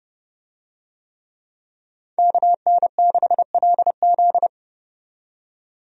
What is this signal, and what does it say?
Morse code, heard as a keyed tone, about 30 wpm, 700 Hz. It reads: KN6LZ
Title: USA Call Signs - 5 Characters - 30wpm